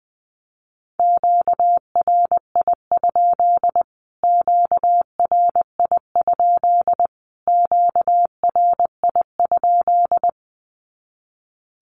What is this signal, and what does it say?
Morse code, heard as a keyed tone, about 19 wpm, 700 Hz. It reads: QRI? QRI? QRI?
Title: Q-Codes Repeated 3x 20wpm